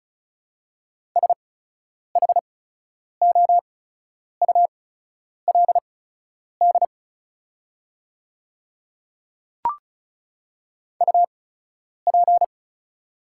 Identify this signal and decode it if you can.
Morse code, heard as a keyed tone, about 36 wpm, spaced as for 10 wpm, 700 Hz. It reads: SHOULD UP
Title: 8x Character Spacing - Top 300 Words - 35wpm